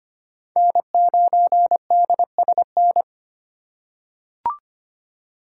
Morse code, heard as a keyed tone, about 25 wpm, 700 Hz. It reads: N9DSN
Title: USA Call Signs - 5 Characters - 4x Character Spacing - 25wpm